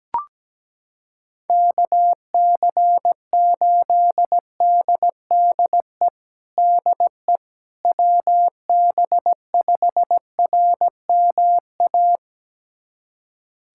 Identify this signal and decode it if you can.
Morse code, heard as a keyed tone, about 17 wpm, 700 Hz. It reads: KC8DDE DE WB5RMA
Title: USA Callsign DE Callsign - 6 Characters - 17wpm